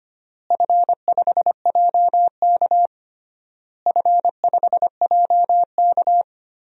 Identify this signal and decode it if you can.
Morse code, heard as a keyed tone, about 25 wpm, 700 Hz. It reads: F5JK F5JK
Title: DXCC-100 Call Signs - 4 Characters - Repeated 3x - 25wpm